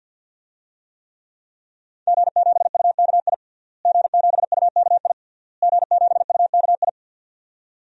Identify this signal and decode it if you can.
Morse code, heard as a keyed tone, about 46 wpm, 700 Hz. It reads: G7UKI G7UKI G7UKI
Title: DXCC-100 Call Signs - 5 Characters - Repeated 3x - 50wpm